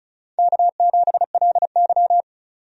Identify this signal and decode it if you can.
Morse code, heard as a keyed tone, about 36 wpm, 700 Hz. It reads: K7LY
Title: USA Call Signs - 4 Characters - 35wpm